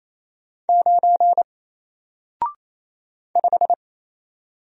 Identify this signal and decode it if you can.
Morse code, heard as a keyed tone, about 27 wpm, 700 Hz. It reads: 9 5
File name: Single-Letter-Number-28wpm